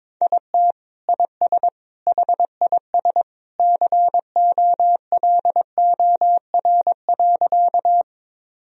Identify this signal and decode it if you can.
Morse code, heard as a keyed tone, about 22 wpm, 700 Hz. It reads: IT IS HIS COLOR.
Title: Sentences from Top 1000 Words - Review Part 2 - 22wpm